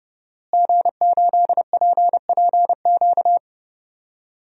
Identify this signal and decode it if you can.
Morse code, heard as a keyed tone, about 31 wpm, 700 Hz. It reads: G8PPQ
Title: DXCC-100 Call Signs - 5 Characters - Repeated 3x - 30wpm